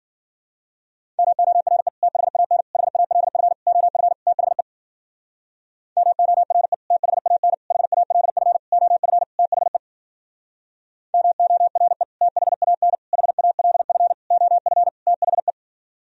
Morse code, heard as a keyed tone, about 52 wpm, 700 Hz. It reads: MORE THAN HALF OF THE MORE THAN HALF OF THE MORE THAN HALF OF THE